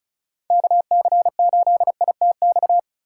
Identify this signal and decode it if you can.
Morse code, heard as a keyed tone, about 34 wpm, 700 Hz. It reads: KC8ITX
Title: USA Call Signs - 6 Characters - 4x Character Spacing - 35wpm